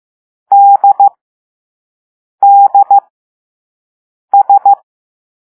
Morse code, heard as a keyed tone, about 15 wpm, spaced as for 5 wpm, 800 Hz. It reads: DDS